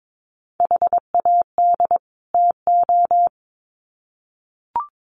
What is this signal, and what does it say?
Morse code, heard as a keyed tone, about 22 wpm, 700 Hz. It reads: HAD TO
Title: Sets of 2 Words 22wpm